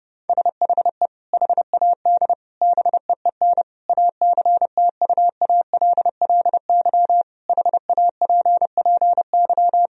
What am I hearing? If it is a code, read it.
Morse code, heard as a keyed tone, about 32 wpm, 700 Hz. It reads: SHE HAD BEEN ACTUALLY HAPPY
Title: Sentences from Top 1000 Words - Review Part 4 - 30wpm